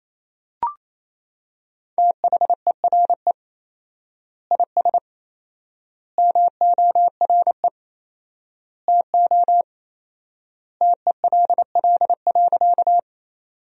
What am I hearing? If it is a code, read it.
Morse code, heard as a keyed tone, about 28 wpm, 700 Hz. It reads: THERE IS MORE TO TELL.
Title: 4x Word-Spacing - Sentences from Top 200 Words - 28wpm